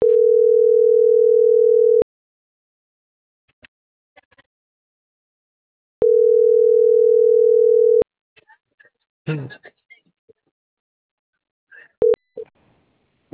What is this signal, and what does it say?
Ringback tone (North America), 440 + 480 Hz.